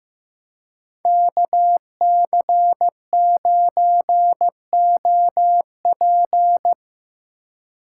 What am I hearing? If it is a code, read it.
Morse code, heard as a keyed tone, about 17 wpm, 700 Hz. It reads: KC9OP
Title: Speed Racing - USA 2x2 Call Signs 15wpm